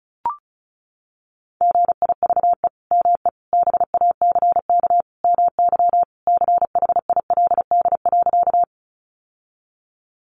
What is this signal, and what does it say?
Morse code, heard as a keyed tone, about 34 wpm, 700 Hz. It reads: GIVE ME BACK MY CHILD.
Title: Sentences from Top 800 Words 35wpm